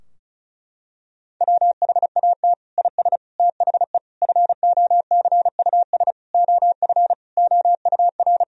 Morse code, heard as a keyed tone, about 36 wpm, 700 Hz. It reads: WHAT IS THE FOCUS OF OUR
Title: Sentences from Top 1000 Words - Review Part 4 - 35wpm